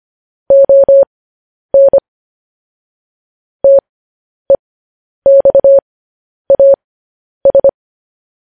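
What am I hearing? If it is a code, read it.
Morse code, heard as a keyed tone, about 25 wpm, spaced as for 10 wpm, 550 Hz. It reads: ON TEXAS
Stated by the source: Morse code practice sending news headlines at 10 wpm.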